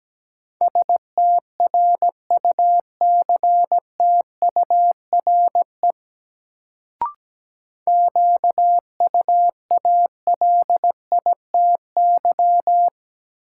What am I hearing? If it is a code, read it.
Morse code, heard as a keyed tone, about 17 wpm, 700 Hz. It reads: STRUCTURE QUALITY
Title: Sentences from Top 1500 Words - New Word Drill Alt - 17wpm